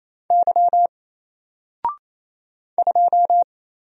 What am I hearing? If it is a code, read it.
Morse code, heard as a keyed tone, about 27 wpm, 700 Hz. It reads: Y 2
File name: Single-Letter-Number-28wpm